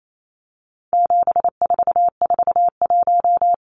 Morse code, heard as a keyed tone, about 28 wpm, 700 Hz. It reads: 7441
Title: Numbers - 4-Digit 28wpm